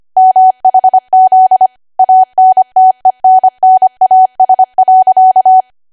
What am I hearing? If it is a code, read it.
Morse code, heard as a keyed tone, about 25 wpm, 750 Hz. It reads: MHZ ANTENNAS.
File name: Archive / 25 WPM / 25WPM